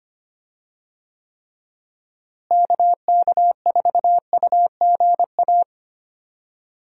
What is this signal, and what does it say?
Morse code, heard as a keyed tone, about 25 wpm, 700 Hz. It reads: KK4UGA
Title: USA Call Signs - 6 Characters - Repeated 3x - 25wpm